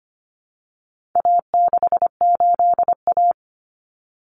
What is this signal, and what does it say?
Morse code, heard as a keyed tone, about 25 wpm, 700 Hz. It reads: A68A
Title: Common Cut Numbers - 4-Digit 25wpm